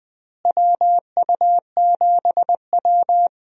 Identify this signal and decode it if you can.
Morse code, heard as a keyed tone, about 20 wpm, 700 Hz. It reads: WU7W
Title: USA Call Signs - 4 Characters - Repeated 2x - 20wpm